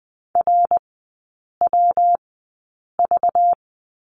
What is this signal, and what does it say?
Morse code, heard as a keyed tone, about 20 wpm, spaced as for 8 wpm, 700 Hz. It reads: RWV
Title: 191 Lesson 31e - Letter v - Three Characters 20wpm